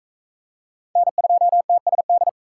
Morse code, heard as a keyed tone, about 44 wpm, 700 Hz. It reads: N2TSD